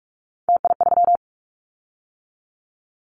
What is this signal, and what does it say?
Morse code, heard as a keyed tone, about 44 wpm, 700 Hz. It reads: TI3
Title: Callsign Prefixes for Top 100 DXCC - 45wpm